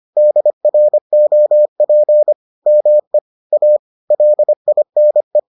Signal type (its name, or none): Morse code